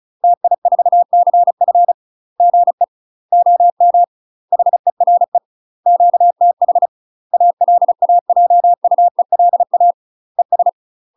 Morse code, heard as a keyed tone, about 36 wpm, 700 Hz. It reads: TI4CF GE OM HERE QTH ALAJUELA ES